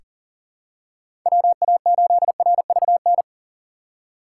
Morse code, heard as a keyed tone, about 40 wpm, 700 Hz. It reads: WA8RVN
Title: USA Call Signs - 6 Characters - 2x Character Spacing - 40wpm